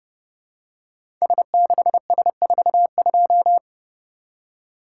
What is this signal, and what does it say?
Morse code, heard as a keyed tone, about 30 wpm, 700 Hz. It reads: S6S42
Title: Code Groups - 5 Letter:Numbers - 30wpm